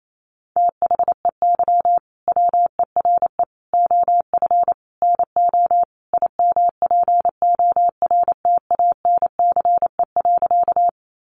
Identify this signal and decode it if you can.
Morse code, heard as a keyed tone, about 28 wpm, 700 Hz. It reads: THEY WERE OF NO IMPORTANCE.
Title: Sentences from Top 1200 Words 28wpm